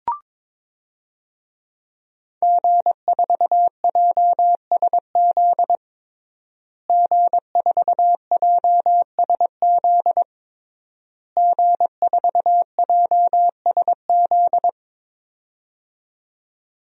Morse code, heard as a keyed tone, about 22 wpm, 700 Hz. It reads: G4JSZ G4JSZ G4JSZ
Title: DXCC-100 Call Signs - 5 Characters - Repeated 3x - 22wpm